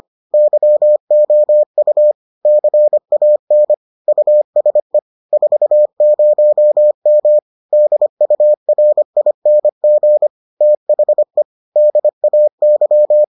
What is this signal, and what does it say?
Morse code, heard as a keyed tone, about 25 wpm, 600 Hz. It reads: YOU CAN USE 40M DURING THE DAY